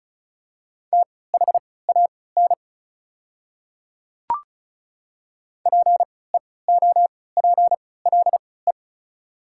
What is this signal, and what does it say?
Morse code, heard as a keyed tone, about 36 wpm, spaced as for 20 wpm, 690 Hz. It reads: THAN PEOPLE
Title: 3x Character Spacing - Top 100 Words - 35wpm